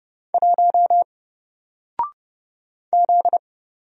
Morse code, heard as a keyed tone, about 31 wpm, 700 Hz. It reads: 1 Z
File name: Single-Letter-Number-30wpm